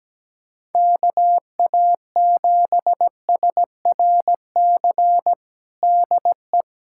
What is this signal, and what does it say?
Morse code, heard as a keyed tone, about 17 wpm, 700 Hz. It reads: KA7SRC DE
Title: USA Callsign DE Callsign - 6 Characters - 17wpm